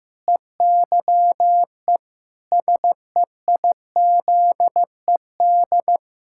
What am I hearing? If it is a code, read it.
Morse code, heard as a keyed tone, about 15 wpm, 700 Hz. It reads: EYE SEIZED
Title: Pangram Challenge v2 - 15wpm